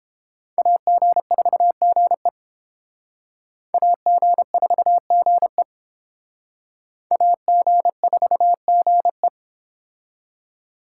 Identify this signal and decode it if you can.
Morse code, heard as a keyed tone, about 28 wpm, 700 Hz. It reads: AG4GE AG4GE AG4GE